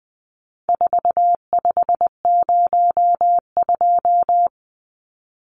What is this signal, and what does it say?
Morse code, heard as a keyed tone, about 20 wpm, 700 Hz. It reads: 4502
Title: Numbers - 4-Digit 20wpm